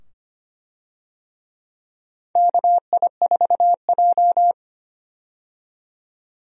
Morse code, heard as a keyed tone, about 25 wpm, 700 Hz. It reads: KI4J
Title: USA Call Signs - 4 Characters - 25wpm